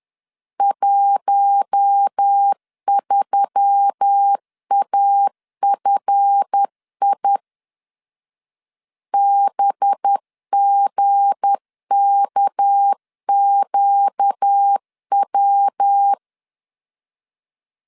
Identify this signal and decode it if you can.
Morse code, heard as a keyed tone, about 10 wpm, 800 Hz. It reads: 13AFI BGKQW
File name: LEZIONI / 7 GF9Z / 7 GF 10wpm 2514